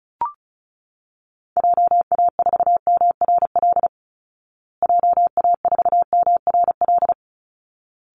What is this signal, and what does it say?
Morse code, heard as a keyed tone, about 34 wpm, 700 Hz. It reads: JA4MRL JA4MRL
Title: Call Signs Repeated 3x - 35wpm